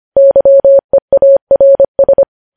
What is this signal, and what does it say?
Morse code, heard as a keyed tone, about 25 wpm, 550 Hz. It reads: YEARS
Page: News Headlines in Morse Code at 25 WPM